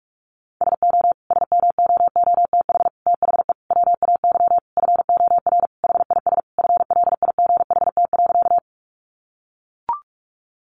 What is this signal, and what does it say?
Morse code, heard as a keyed tone, about 44 wpm, 700 Hz. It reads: SO SMOOTH THE WAY FOR HIS FLIGHT.
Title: Sentences from Top 1400 Words 45wpm